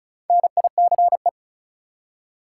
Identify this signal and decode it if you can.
Morse code, heard as a keyed tone, about 36 wpm, 700 Hz. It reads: NICE
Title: Four Letter Words - 35wpm